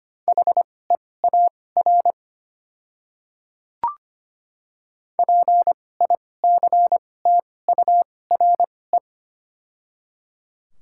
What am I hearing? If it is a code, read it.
Morse code, heard as a keyed tone, about 25 wpm, spaced as for 18 wpm, 690 Hz. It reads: HEAR PICTURE
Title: 2x Character Spacing - Top 300 Words - 25wpm